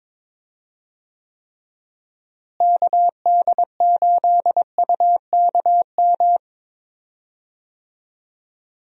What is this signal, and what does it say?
Morse code, heard as a keyed tone, about 22 wpm, 700 Hz. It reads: KD8UKM
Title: USA Call Signs - 6 Characters - Repeated 2x - 22wpm